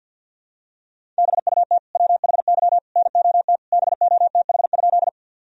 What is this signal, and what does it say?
Morse code, heard as a keyed tone, about 51 wpm, 700 Hz. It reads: BUT WHY NOT BOTH?
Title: Sentences from Top 1000 Words - Review Part 1 - 50wpm